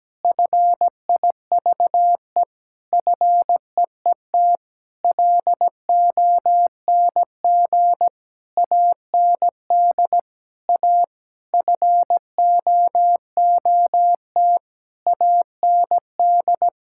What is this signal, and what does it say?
Morse code, heard as a keyed tone, about 17 wpm, 700 Hz. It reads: FIVE FEET LONG AND A FOOT AND
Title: Sentences from Top 1000 Words - Review Part 6 - 17wpm